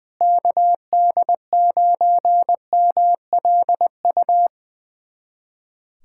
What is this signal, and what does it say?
Morse code, heard as a keyed tone, about 20 wpm, 700 Hz. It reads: KD9MLU
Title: USA Callsign DE Callsign - 6 Characters - 20wpm